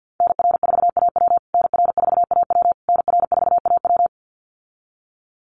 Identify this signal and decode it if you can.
Morse code, heard as a keyed tone, about 50 wpm, 700 Hz. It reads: NR4AW NR4AW NR4AW
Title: USA Call Signs Repeated 3x 50wpm